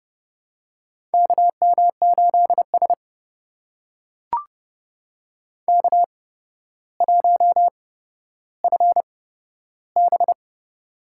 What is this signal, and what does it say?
Morse code, heard as a keyed tone, about 32 wpm, spaced as for 13 wpm, 700 Hz. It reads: KM8S K1FB
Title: USA Call Signs - 4 Characters - 8x Character Spacing - 30wpm